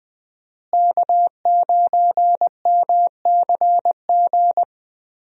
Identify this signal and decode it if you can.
Morse code, heard as a keyed tone, about 20 wpm, 700 Hz. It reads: K9MCG
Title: USA Call Signs - 5 Characters - Repeated 2x - 20wpm